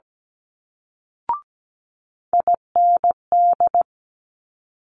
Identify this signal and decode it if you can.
Morse code, heard as a keyed tone, about 17 wpm, 700 Hz. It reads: IND
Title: Multi-letter Phonemes Quick Fire 17wpm